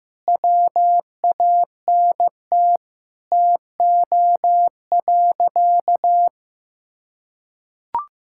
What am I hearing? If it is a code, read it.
Morse code, heard as a keyed tone, about 15 wpm, 700 Hz. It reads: WANT TO.
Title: Sentences from Top 1600 Words 15wpm